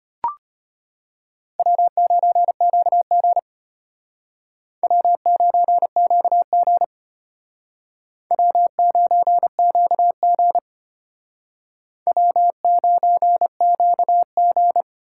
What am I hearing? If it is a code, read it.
Morse code, heard as a keyed tone, about 31 wpm, 700 Hz. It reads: W9QG W9QG W9QG W9QG
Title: Speed Racing - USA Call Signs 25wpm